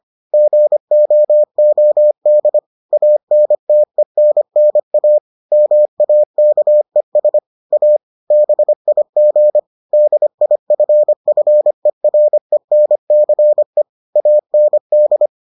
Morse code, heard as a keyed tone, about 25 wpm, 600 Hz. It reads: GOOD ANTENNA MAKES A BIG DIFFERENCE AND